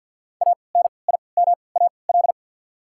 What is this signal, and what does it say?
Morse code, heard as a keyed tone, about 54 wpm, spaced as for 29 wpm, 690 Hz. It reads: ANIMAL
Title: 3x Character Spacing - Top 300 Words - 50wpm